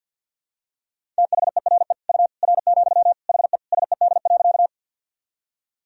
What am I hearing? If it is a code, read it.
Morse code, heard as a keyed tone, about 47 wpm, 700 Hz. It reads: THERE U R, HE SED.
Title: Sentences from Top 100 Words and Abbreviations 50wpm